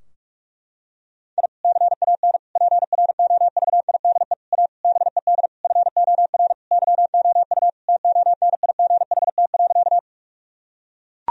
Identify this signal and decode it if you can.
Morse code, heard as a keyed tone, about 41 wpm, 700 Hz. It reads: I CAN PROVIDE A BED FOR YOU TONIGHT.
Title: Sentences from Top 1000 Words - Review Part 3 - 45wpm